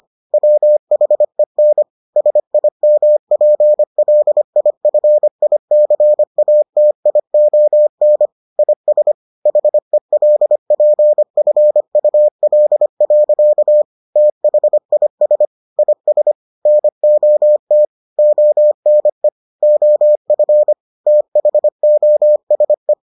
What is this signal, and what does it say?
Morse code, heard as a keyed tone, about 25 wpm, 600 Hz. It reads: WHEN SIMPLIFICATION IS HELPFUL. THIS IS NOT ONE OF THOSE